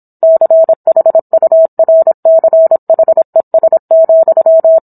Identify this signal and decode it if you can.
Morse code, heard as a keyed tone, about 26 wpm, 650 Hz. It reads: CHURCHES,